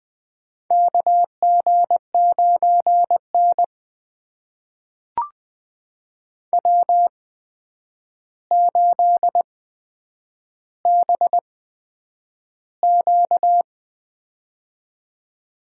Morse code, heard as a keyed tone, about 20 wpm, spaced as for 9 wpm, 700 Hz. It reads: KG9N W8BQ
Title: USA Call Signs - 4 Characters - 8x Character Spacing - 20wpm